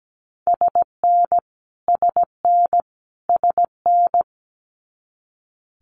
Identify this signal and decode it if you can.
Morse code, heard as a keyed tone, about 17 wpm, 700 Hz. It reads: SN SN SN
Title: Multi-letter Phonemes Repeated 3x 17wpm